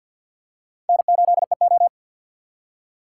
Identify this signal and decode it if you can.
Morse code, heard as a keyed tone, about 53 wpm, 700 Hz. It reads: N8EO
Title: USA Call Signs - 4 Characters - 6x Character Spacing - 50wpm